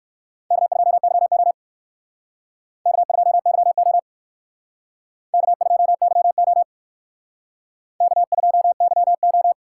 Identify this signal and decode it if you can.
Morse code, heard as a keyed tone, about 56 wpm, 700 Hz. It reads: K2YO K2YO K2YO K2YO